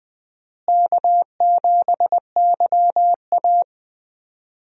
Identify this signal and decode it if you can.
Morse code, heard as a keyed tone, about 20 wpm, 700 Hz. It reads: K7YA
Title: USA Call Signs - 4 Characters - Repeated 2x - 20wpm